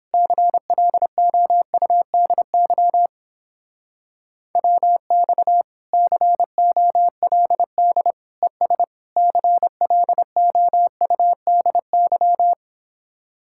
Morse code, heard as a keyed tone, about 31 wpm, 700 Hz. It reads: CLOUDY WX COLD ES CLOUDY